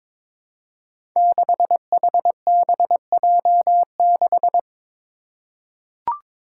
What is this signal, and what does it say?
Morse code, heard as a keyed tone, about 22 wpm, 700 Hz. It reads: 6HBJ6
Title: Five Challenging Letter-Number Combinations 22wpm